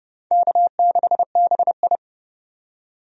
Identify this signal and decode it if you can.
Morse code, heard as a keyed tone, about 30 wpm, 700 Hz. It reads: K6BI
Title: USA Call Signs - 4 Characters - 30wpm